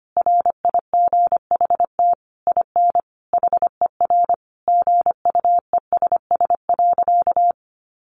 Morse code, heard as a keyed tone, about 25 wpm, 700 Hz. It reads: RIGHT IN HER GUESS.
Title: Sentences from Top 1500 Words 25wpm